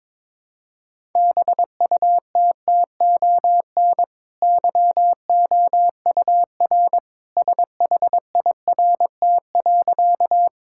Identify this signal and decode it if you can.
Morse code, heard as a keyed tone, about 21 wpm, 700 Hz. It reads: BUTTON YOUR SHIRT.
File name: practice / 22wpm / Easy Sentences with Hints - 22wpm